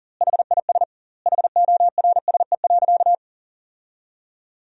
Morse code, heard as a keyed tone, about 40 wpm, 700 Hz. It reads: HIS HORSE.
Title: 2x Word-Spacing - Sentences from Top 300 Words - 40wpm